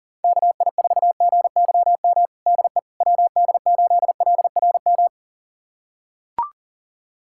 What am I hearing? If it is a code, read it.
Morse code, heard as a keyed tone, about 40 wpm, 700 Hz. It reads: KI4GYM DE WD8LRM
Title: USA Callsign DE Callsign - 6 Characters - 40wpm